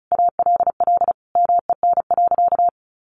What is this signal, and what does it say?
Morse code, heard as a keyed tone, about 34 wpm, 700 Hz. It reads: ALL MEN.
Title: Sentences from Top 1200 Words 35wpm